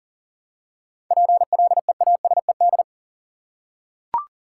Morse code, heard as a keyed tone, about 40 wpm, 700 Hz. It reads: PLEASED
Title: Seven Letter Words - 40wpm